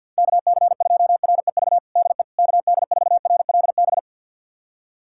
Morse code, heard as a keyed tone, about 49 wpm, 700 Hz. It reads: KC2REV DE KD4RLB
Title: USA Callsign DE Callsign - 6 Characters - 50wpm